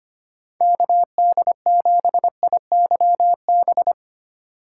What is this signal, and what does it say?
Morse code, heard as a keyed tone, about 25 wpm, 700 Hz. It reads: KD7IYB
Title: USA Call Signs - 6 Characters - 3x Character Spacing - 25wpm